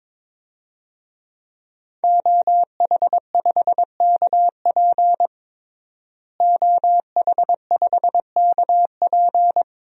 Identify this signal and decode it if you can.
Morse code, heard as a keyed tone, about 22 wpm, 700 Hz. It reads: OH5KP OH5KP
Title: DXCC-100 Call Signs - 5 Characters - Repeated 2x - 22wpm